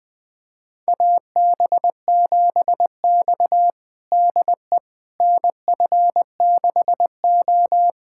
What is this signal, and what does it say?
Morse code, heard as a keyed tone, about 20 wpm, 700 Hz. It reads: AB7X DE NF6O
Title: USA Callsign DE Callsign - 4 Characters - 20wpm